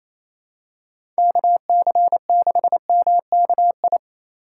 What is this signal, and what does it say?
Morse code, heard as a keyed tone, about 28 wpm, 700 Hz. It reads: KC6MKI
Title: USA Call Signs - 6 Characters - 3x Character Spacing - 28wpm